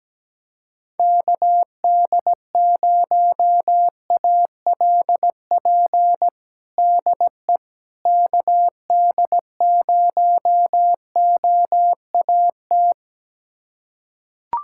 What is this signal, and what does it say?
Morse code, heard as a keyed tone, about 17 wpm, 700 Hz. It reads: KD0ALP DE KD0OAT
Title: USA Callsign DE Callsign 17wpm